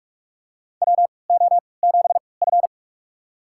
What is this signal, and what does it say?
Morse code, heard as a keyed tone, about 45 wpm, spaced as for 25 wpm, 700 Hz. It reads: WO7F
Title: USA Call Signs - 4 Characters - 3x Character Spacing - 45wpm